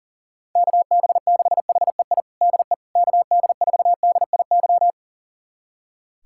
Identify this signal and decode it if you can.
Morse code, heard as a keyed tone, about 40 wpm, 700 Hz. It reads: KB6HEI DE KD4DIY
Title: USA Callsign DE Callsign - 6 Characters - 40wpm